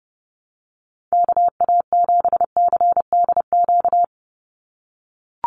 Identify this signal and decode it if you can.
Morse code, heard as a keyed tone, about 30 wpm, 700 Hz. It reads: KA7CDQ
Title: USA Call Signs 30wpm